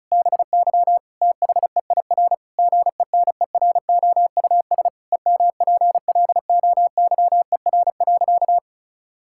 Morse code, heard as a keyed tone, about 36 wpm, 700 Hz. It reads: BY THEIR GENEROUS EMPLOYER.
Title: Pangram Challenge v2 - 35wpm